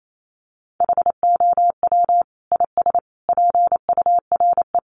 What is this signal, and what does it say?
Morse code, heard as a keyed tone, about 28 wpm, 700 Hz. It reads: HOW IS PURE
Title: Sentences from Top 1500 Words 28wpm